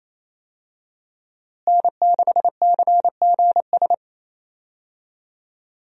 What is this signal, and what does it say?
Morse code, heard as a keyed tone, about 28 wpm, 700 Hz. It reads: N6CGS
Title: USA Call Signs - 5 Characters - Repeated 3x - 28wpm